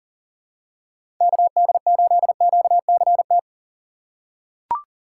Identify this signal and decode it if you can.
Morse code, heard as a keyed tone, about 40 wpm, 690 Hz. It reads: KD8QCT
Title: USA Call Signs - 4x Character Spacing - 40wpm